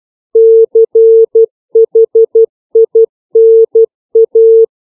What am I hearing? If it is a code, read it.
Morse code, heard as a keyed tone, about 12 wpm, 450 Hz. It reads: CHINA
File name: CW / five common words